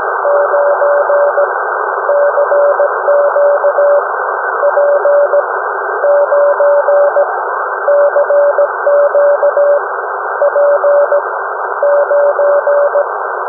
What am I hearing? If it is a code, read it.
Morse code, heard as a keyed tone, about 17 wpm, spaced as for 12 wpm, 580 Hz, with noise. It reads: 9 CQ P9 CQ P9
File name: UM01- -0743utc-3581khz